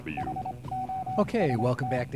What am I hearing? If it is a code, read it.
Morse code, heard as a keyed tone, about 25 wpm, 760 Hz, with speech and radio noise. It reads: HOST